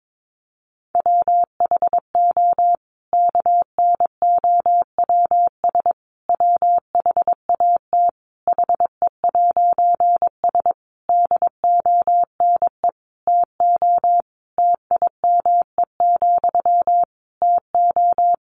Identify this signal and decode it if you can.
Morse code, heard as a keyed tone, about 22 wpm, 700 Hz. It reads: WHO KNOWS WHAT HE'S DONE TO TIME, TO
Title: A Sound Of Thunder 22wpm